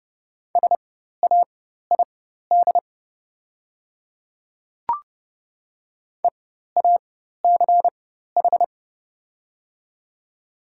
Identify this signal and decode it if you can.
Morse code, heard as a keyed tone, about 32 wpm, spaced as for 14 wpm, 700 Hz. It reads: SAID EACH
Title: 4x Character Spacing - Top 200 Words - 30wpm